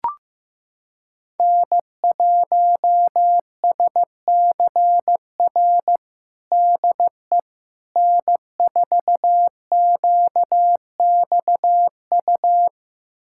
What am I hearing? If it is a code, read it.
Morse code, heard as a keyed tone, about 15 wpm, 700 Hz. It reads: N1SCR DE N4QXU
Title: USA Callsign DE Callsign - 5 Characters - 15wpm